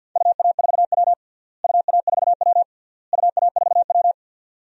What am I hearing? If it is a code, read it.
Morse code, heard as a keyed tone, about 53 wpm, 700 Hz. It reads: UA4W UA4W UA4W
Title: DXCC-100 Call Signs - 4 Characters - Repeated 3x - 50wpm